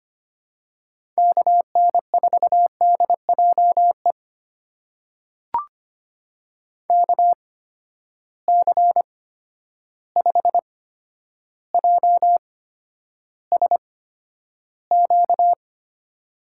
Morse code, heard as a keyed tone, about 25 wpm, spaced as for 11 wpm, 700 Hz. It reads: KN4DJE KC5JSQ
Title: USA Call Signs - 6 Characters - 8x Character Spacing - 25wpm